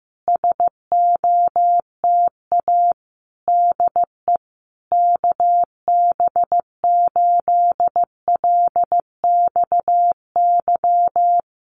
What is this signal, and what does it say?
Morse code, heard as a keyed tone, about 15 wpm, 700 Hz. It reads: SOTA DE KB8LXY
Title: Common QSO Phrases Encore 15wpm